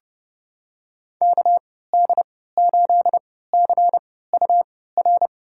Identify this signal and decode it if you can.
Morse code, heard as a keyed tone, about 31 wpm, spaced as for 17 wpm, 700 Hz. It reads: KD8CUR
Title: USA Call Signs - 3x Character Spacing - 30wpm